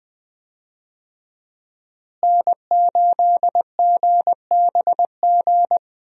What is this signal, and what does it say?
Morse code, heard as a keyed tone, about 21 wpm, 700 Hz. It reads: N8GBG
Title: USA Call Signs - 5 Characters - Repeated 3x - 20wpm